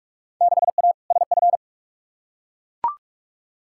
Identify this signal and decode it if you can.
Morse code, heard as a keyed tone, about 45 wpm, 700 Hz. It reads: 6A SF
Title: Field Day Exchanges - 45wpm